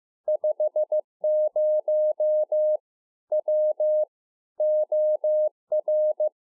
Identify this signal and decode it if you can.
Morse code, heard as a keyed tone, about 15 wpm, 610 Hz. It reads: 50 W OR